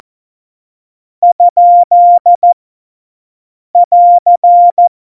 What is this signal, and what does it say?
Morse code, heard as a keyed tone, about 14 wpm, 700 Hz. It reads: ? <AR>
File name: Morsekurs / Witze / Witz 28. August 2018, 50 F70